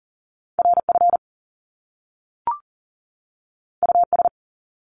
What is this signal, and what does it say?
Morse code, heard as a keyed tone, about 41 wpm, 700 Hz. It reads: RF US
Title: Top Two Letter Combinations in Words 40wpm